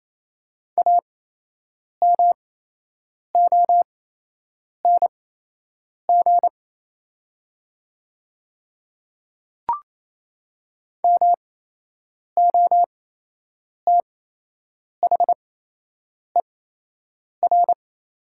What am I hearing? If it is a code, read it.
Morse code, heard as a keyed tone, about 28 wpm, spaced as for 8 wpm, 700 Hz. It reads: AMONG MOTHER
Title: 8x Character Spacing - Top 500 Words - 28wpm